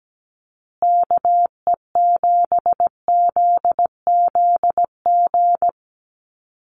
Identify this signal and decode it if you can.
Morse code, heard as a keyed tone, about 17 wpm, 700 Hz. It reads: KE7ZZG
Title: USA Call Signs Repeated 2x 17wpm